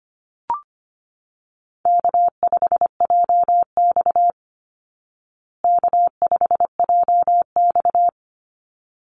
Morse code, heard as a keyed tone, about 25 wpm, 700 Hz. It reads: K5JX K5JX
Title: Easy Call Signs Repeated 2x - 25wpm